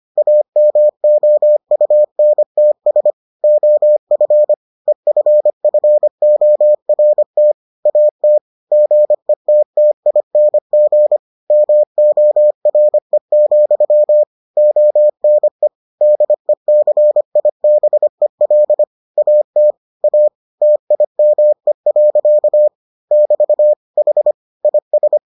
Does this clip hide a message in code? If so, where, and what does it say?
Yes: Morse code from 0.2 to 25.2 s, heard as a keyed tone, about 25 wpm, 600 Hz. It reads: AMOUNTS OF EFFORT AT GETTING MORE, ONE DECIBEL AT A TIME. <BT> H IS